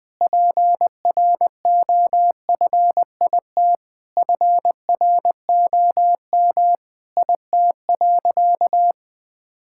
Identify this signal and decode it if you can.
Morse code, heard as a keyed tone, about 21 wpm, 700 Hz. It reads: PROFIT FROM IT.
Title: Sentences from Top 1000 Words - Review Part 5 - 20wpm